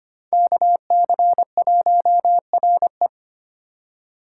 Morse code, heard as a keyed tone, about 25 wpm, 700 Hz. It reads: KC1RE
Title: USA Call Signs - 3x Character Spacing - 25wpm